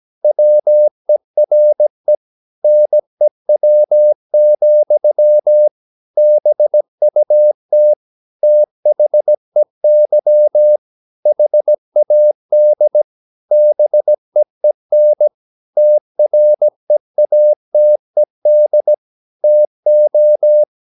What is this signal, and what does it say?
Morse code, heard as a keyed tone, about 17 wpm, 600 Hz. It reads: WERE NEW, BUT THEY HAD BEEN TREATED TO
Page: Romanen Never af Ken Follet på engelsk.